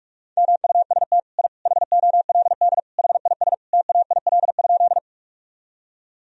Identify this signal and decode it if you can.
Morse code, heard as a keyed tone, about 41 wpm, 700 Hz. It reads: MUST I HOLD HIS TAIL?
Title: Sentences from Top 1000 Words - Review Part 3 - 45wpm